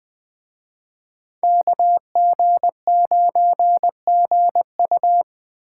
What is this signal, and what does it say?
Morse code, heard as a keyed tone, about 20 wpm, 700 Hz. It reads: KG9GU
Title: Speed Racing - USA 2x2 Call Signs 17wpm